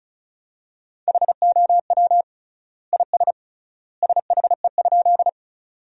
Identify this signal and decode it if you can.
Morse code, heard as a keyed tone, about 36 wpm, 700 Hz. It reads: HOW IS SHE?
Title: 3x Word-Spacing - Sentences from Top 100 Words - 35wpm